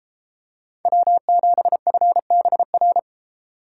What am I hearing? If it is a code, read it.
Morse code, heard as a keyed tone, about 33 wpm, 700 Hz. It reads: W7FBR